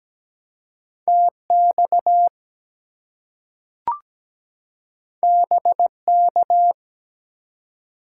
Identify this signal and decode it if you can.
Morse code, heard as a keyed tone, about 17 wpm, 700 Hz. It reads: TX BK
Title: Top 90 QSO Elements 17wpm